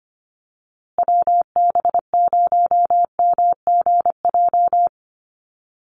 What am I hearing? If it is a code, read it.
Morse code, heard as a keyed tone, about 25 wpm, 700 Hz. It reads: WB0MGJ
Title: USA Call Signs Repeated 2x 25wpm